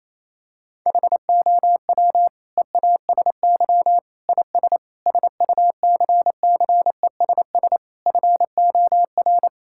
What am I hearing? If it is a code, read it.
Morse code, heard as a keyed tone, about 28 wpm, 700 Hz. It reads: HOW EASY IS SUCCESS FOR
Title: Sentences from Top 1000 Words - Review Part 6 - 28wpm